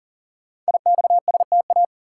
Morse code, heard as a keyed tone, about 44 wpm, 700 Hz. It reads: IXSTA
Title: Code Groups - 5 Letters - 40wpm